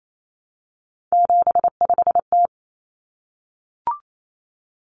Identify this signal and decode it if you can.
Morse code, heard as a keyed tone, about 28 wpm, 700 Hz. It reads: 75T
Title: Common Cut Numbers - 3-Digit 28wpm